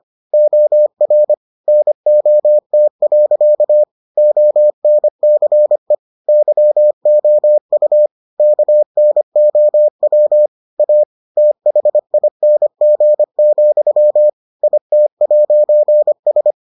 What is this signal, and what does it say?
Morse code, heard as a keyed tone, about 25 wpm, 600 Hz. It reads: OR NOT. ONCE YOU KNOW A THING, IT'S